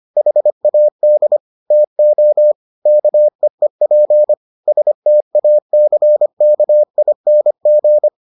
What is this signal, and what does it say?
Morse code, heard as a keyed tone, about 25 wpm, 600 Hz. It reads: HAD TO KEEP STACKING